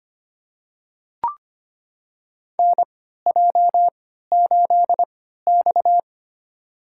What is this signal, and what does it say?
Morse code, heard as a keyed tone, about 25 wpm, spaced as for 14 wpm, 700 Hz. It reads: NJ8X
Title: USA Call Signs - 4 Characters - 3x Character Spacing - 25wpm